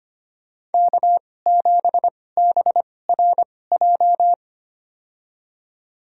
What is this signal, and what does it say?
Morse code, heard as a keyed tone, about 25 wpm, spaced as for 18 wpm, 700 Hz. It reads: K7BRJ
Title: USA Call Signs - 2x Character Spacing - 25wpm